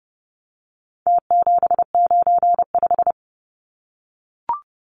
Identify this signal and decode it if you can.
Morse code, heard as a keyed tone, about 30 wpm, 700 Hz. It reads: T795
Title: Numbers - Contest Style - 4-Digit 30wpm